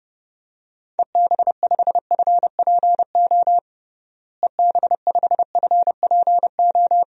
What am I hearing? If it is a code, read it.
Morse code, heard as a keyed tone, about 30 wpm, 700 Hz. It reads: EB5FPO EB5FPO
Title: DXCC-100 Call Signs - 6 Characters - Repeated 2x - 30wpm